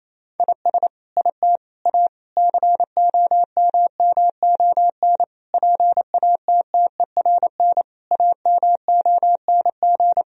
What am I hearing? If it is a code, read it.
Morse code, heard as a keyed tone, about 28 wpm, 700 Hz. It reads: IS IT A COMMON PATTERN AMONG
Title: Sentences from Top 1000 Words - Review Part 2 - 28wpm